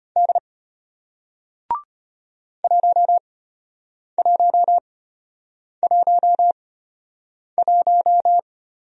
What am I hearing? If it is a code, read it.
Morse code, heard as a keyed tone, about 33 wpm, spaced as for 8 wpm, 700 Hz. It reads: D 1111